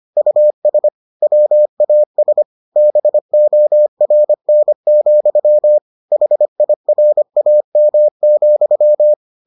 Morse code, heard as a keyed tone, about 25 wpm, 600 Hz. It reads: US WAS BORN, HIRAM,